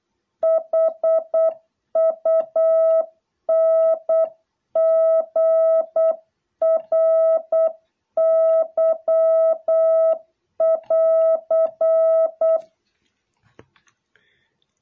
Morse code, heard as a keyed tone, about 8 wpm, 660 Hz. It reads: HUNGRY<AR>